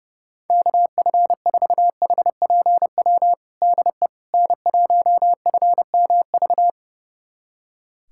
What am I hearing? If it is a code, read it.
Morse code, heard as a keyed tone, about 32 wpm, 700 Hz. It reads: KF4HPW DE N1FMV
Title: USA Callsign DE Callsign 30wpm